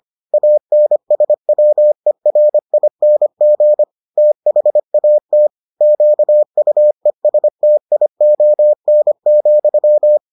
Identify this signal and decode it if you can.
Morse code, heard as a keyed tone, about 25 wpm, 600 Hz. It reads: ANSWERING THAT QUESTION,